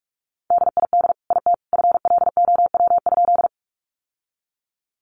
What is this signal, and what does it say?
Morse code, heard as a keyed tone, about 44 wpm, 700 Hz. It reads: DID IT FLOW?
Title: Sentences from Top 1200 Words 45wpm